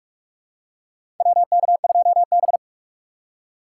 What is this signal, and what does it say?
Morse code, heard as a keyed tone, about 46 wpm, 700 Hz. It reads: WK2B
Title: USA Call Signs - 4x Character Spacing - 45wpm